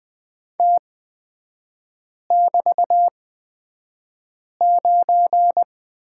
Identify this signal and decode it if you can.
Morse code, heard as a keyed tone, about 20 wpm, spaced as for 5 wpm, 690 Hz. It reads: T<BT>9
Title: 236 Lesson 38e - Letter q - Three Characters 20wpm